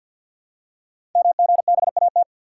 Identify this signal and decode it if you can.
Morse code, heard as a keyed tone, about 52 wpm, 700 Hz. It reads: MGBAT
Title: Code Groups - 5 Letter:Numbers - 50wpm